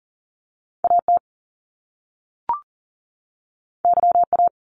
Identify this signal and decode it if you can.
Morse code, heard as a keyed tone, about 41 wpm, 700 Hz. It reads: AT YA
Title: Top Two Letter Combinations in Words 40wpm